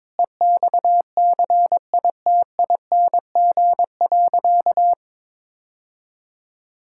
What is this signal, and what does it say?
Morse code, heard as a keyed tone, about 22 wpm, 700 Hz. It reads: EXCITING.
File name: practice / 22wpm / Sentences with 4-Word Phrases - 22wpm